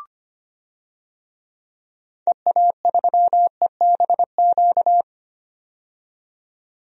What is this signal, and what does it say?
Morse code, heard as a keyed tone, about 25 wpm, 700 Hz. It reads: EA3EBQ
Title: DXCC-100 Call Signs - 6 Characters - 25wpm